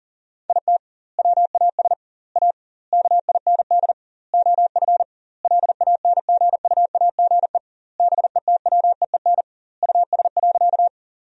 Morse code, heard as a keyed tone, about 40 wpm, 700 Hz. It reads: IT WAS A KIND OF LANGUAGE BETWEEN US.
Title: 2x Word-Spacing - Sentences from Top 500 Words - 40wpm